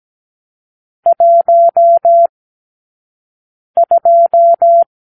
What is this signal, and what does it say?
Morse code, heard as a keyed tone, about 17 wpm, 680 Hz. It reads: 1 2
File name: k7qo / Lesson 44 (1-3)